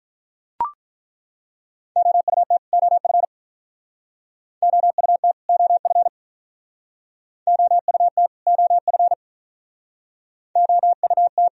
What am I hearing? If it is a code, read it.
Morse code, heard as a keyed tone, about 43 wpm, 700 Hz. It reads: OUT OF OUT OF OUT OF OUT
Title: Speed-Racing - Sets of 2 Words 35wpm